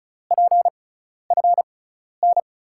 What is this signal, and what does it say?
Morse code, heard as a keyed tone, about 36 wpm, spaced as for 12 wpm, 700 Hz. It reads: PFN
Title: USA Call Signs - 6 Characters - 6x Character Spacing - 35wpm